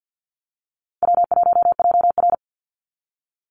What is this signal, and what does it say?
Morse code, heard as a keyed tone, about 50 wpm, 700 Hz. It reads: W1JR
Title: Call Signs Repeated 2x - 50wpm